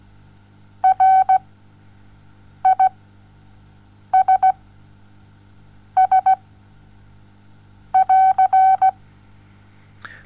Morse code, heard as a keyed tone, about 17 wpm, spaced as for 5 wpm, 760 Hz. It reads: RISS<AR>
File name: Onlinelehrgang / morsen / DJ4UF Morsen07